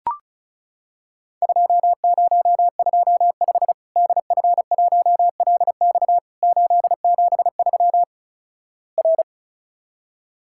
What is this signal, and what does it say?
Morse code, heard as a keyed tone, about 36 wpm, 700 Hz. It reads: 2025 DF1LX 873 R
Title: Worked All Europe DX Contest - QTC Exchange - 35wpm